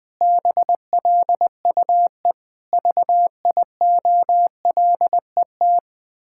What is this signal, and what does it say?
Morse code, heard as a keyed tone, about 20 wpm, 700 Hz. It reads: BLUE VIOLET
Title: Pangram Challenge v2 - 20wpm